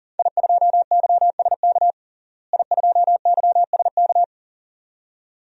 Morse code, heard as a keyed tone, about 40 wpm, 700 Hz. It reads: I2YSK I2YSK
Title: DXCC-100 Call Signs - 5 Characters - Repeated 2x - 40wpm